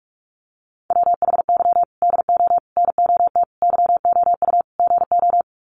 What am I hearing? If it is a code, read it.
Morse code, heard as a keyed tone, about 44 wpm, 700 Hz. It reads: WHY DO NOT YOU GO
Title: Sets of 5 Words Encore 45wpm